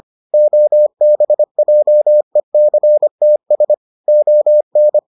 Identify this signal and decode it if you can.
Morse code, heard as a keyed tone, about 25 wpm, 600 Hz. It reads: OBJECTS ON